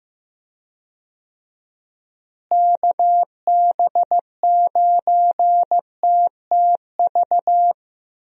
Morse code, heard as a keyed tone, about 16 wpm, 700 Hz. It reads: KB9TTV
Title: USA Call Signs - 6 Characters - Repeated 3x - 15wpm